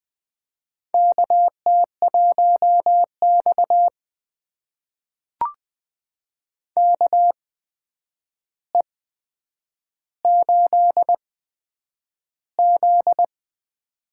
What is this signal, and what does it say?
Morse code, heard as a keyed tone, about 20 wpm, spaced as for 9 wpm, 700 Hz. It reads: KT1X KE8Z
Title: USA Call Signs - 4 Characters - 8x Character Spacing - 20wpm